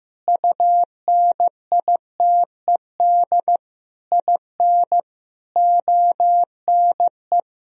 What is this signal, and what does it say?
Morse code, heard as a keyed tone, about 15 wpm, 700 Hz. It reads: UNITED IN ONE
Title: Sentences from Top 1000 Words - Review Part 4 - 15wpm